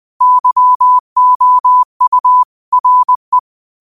Morse code, heard as a keyed tone, about 20 wpm, 1000 Hz. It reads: YOU RE